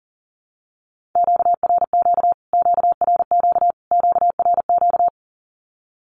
Morse code, heard as a keyed tone, about 40 wpm, 700 Hz. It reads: QRQ QRQ QRQ
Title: Q-Codes Repeated 3x 40wpm